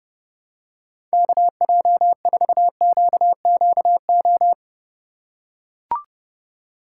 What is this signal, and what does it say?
Morse code, heard as a keyed tone, about 32 wpm, 700 Hz. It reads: KJ4QQO
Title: USA Call Signs - 6 Characters - 4x Character Spacing - 30wpm